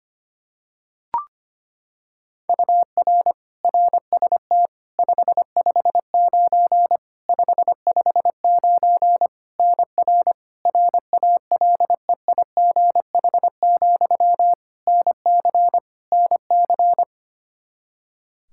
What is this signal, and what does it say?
Morse code, heard as a keyed tone, about 25 wpm, 700 Hz. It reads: UR RST 559 559 NR RALEIGH, NC NC
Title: K1USN SST QSO Practice 25wpm